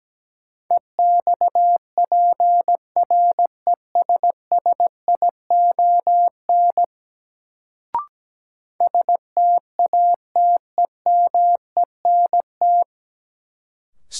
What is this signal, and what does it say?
Morse code, heard as a keyed tone, about 17 wpm, 700 Hz. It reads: EXPRESSION STATEMENT
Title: Sentences from Top 1200 Words - New Word Drill Alt - 17wpm